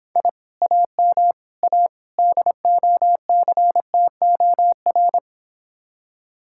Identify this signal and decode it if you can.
Morse code, heard as a keyed tone, about 26 wpm, 700 Hz. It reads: I AM A DOCTOR